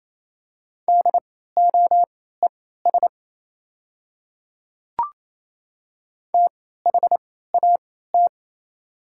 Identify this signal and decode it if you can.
Morse code, heard as a keyed tone, about 28 wpm, spaced as for 16 wpm, 700 Hz. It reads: DOES THAT
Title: 3x Character Spacing - Top 200 Words - 28wpm